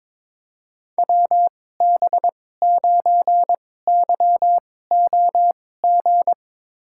Morse code, heard as a keyed tone, about 22 wpm, spaced as for 16 wpm, 700 Hz. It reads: WB9YOG
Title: USA Call Signs - 6 Characters - 2x Character Spacing - 22wpm